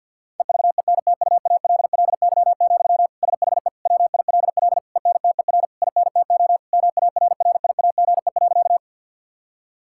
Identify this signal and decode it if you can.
Morse code, heard as a keyed tone, about 50 wpm, 700 Hz. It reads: EVENTUALLY, SHE WILL ENTER INTO MARRIAGE.
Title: Sentences from Top 1000 Words - Review Part 6 - 50wpm